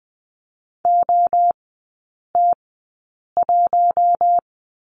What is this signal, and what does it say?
Morse code, heard as a keyed tone, about 20 wpm, spaced as for 8 wpm, 700 Hz. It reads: OT1
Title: 081 Lesson 14e - Number 2 - Three Characters 20wpm